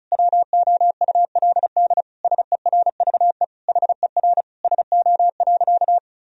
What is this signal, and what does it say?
Morse code, heard as a keyed tone, about 36 wpm, 700 Hz. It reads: WOULD SERVE HER SO.
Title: Sentences from Top 1000 Words - Review Part 2 - 35wpm